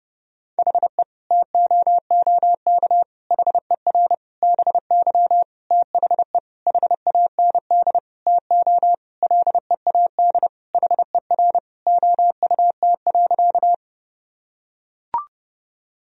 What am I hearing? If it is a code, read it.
Morse code, heard as a keyed tone, about 30 wpm, 700 Hz. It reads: HE TOOK HER BY THE HAND TO LEAD HER OUT.
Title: Sentences from Top 1000 Words - Review Part 2 - 30wpm